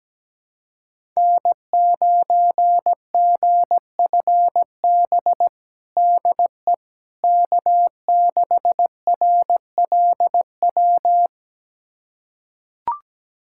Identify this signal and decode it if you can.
Morse code, heard as a keyed tone, about 17 wpm, 700 Hz. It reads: N9GFB DE K6RLW
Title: USA Callsign DE Callsign - 5 Characters - 17wpm